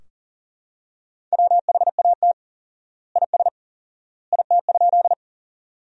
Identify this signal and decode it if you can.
Morse code, heard as a keyed tone, about 40 wpm, 700 Hz. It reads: WHAT IS IT?
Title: 4x Word-Spacing - Sentences from Top 100 Words - 40wpm